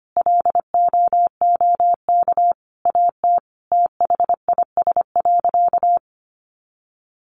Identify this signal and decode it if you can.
Morse code, heard as a keyed tone, about 25 wpm, 700 Hz. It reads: LOOK AT THIS.
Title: Sentences from Top 300 Words 25wpm